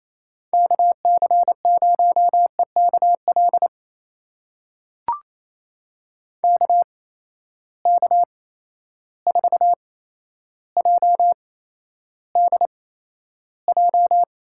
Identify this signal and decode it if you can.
Morse code, heard as a keyed tone, about 28 wpm, spaced as for 12 wpm, 700 Hz. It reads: KC0EKL KK4JDJ
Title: USA Call Signs - 6 Characters - 8x Character Spacing - 28wpm